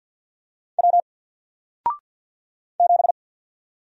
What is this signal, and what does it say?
Morse code, heard as a keyed tone, about 48 wpm, 700 Hz. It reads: W 7
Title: Single Character (Letter:Number:Punctuation:Prosign) Alt 50wpm